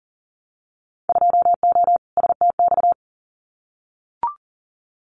Morse code, heard as a keyed tone, about 40 wpm, 700 Hz. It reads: 2O STX
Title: Winter Field Day Exchanges 40wpm